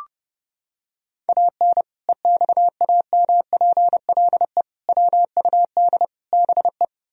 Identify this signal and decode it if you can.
Morse code, heard as a keyed tone, about 30 wpm, 700 Hz. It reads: AN EXAMPLE WUD BE
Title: Sentences from Top 400 Words and Abbreviations 30wpm